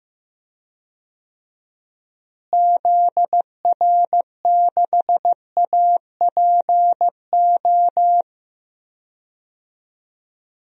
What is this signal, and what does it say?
Morse code, heard as a keyed tone, about 15 wpm, 700 Hz. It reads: ZR6APO
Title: DXCC-100 Call Signs - 6 Characters - Repeated 3x - 15wpm